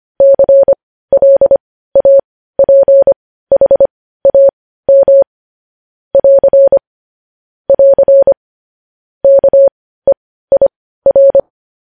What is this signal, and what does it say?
Morse code, heard as a keyed tone, about 25 wpm, spaced as for 15 wpm, 550 Hz. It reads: CLAPHAM <AR> <AR> KEIR
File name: podcast / News headlines-1x-25x15